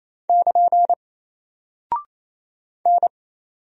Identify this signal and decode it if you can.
Morse code, heard as a keyed tone, about 28 wpm, 700 Hz. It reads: <KN> N
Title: Single Character (Letter:Number:Punctuation:Prosign) Alt 28wpm